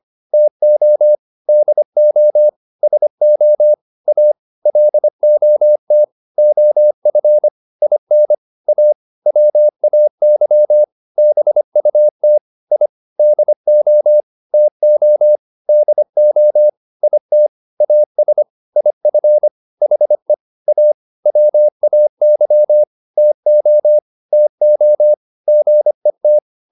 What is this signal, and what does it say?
Morse code, heard as a keyed tone, about 25 wpm, 600 Hz. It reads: TO DO SO A LOT OF IN A WAY BUT I DO TO DO IT AS IF HE A WAY TO TO GET